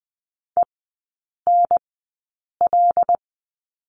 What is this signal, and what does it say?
Morse code, heard as a keyed tone, about 20 wpm, spaced as for 8 wpm, 690 Hz. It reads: ENL
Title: 093 Lesson 16e - Letter c - Three Characters 20wpm